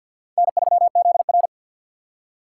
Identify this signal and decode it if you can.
Morse code, heard as a keyed tone, about 65 wpm, spaced as for 57 wpm, 700 Hz. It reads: N3ZR
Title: Speed Racing - USA 1x2 Call Signs 50wpm